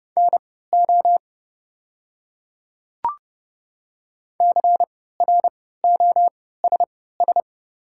Morse code, heard as a keyed tone, about 30 wpm, spaced as for 17 wpm, 690 Hz. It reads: NO CROSS
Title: 3x Character Spacing - Top 300 Words - 30wpm